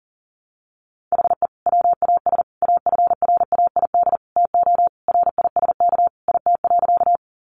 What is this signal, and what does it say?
Morse code, heard as a keyed tone, about 40 wpm, 700 Hz. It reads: HE WAS AFRAID TO RISK IT.
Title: Sentences from Top 1500 Words 40wpm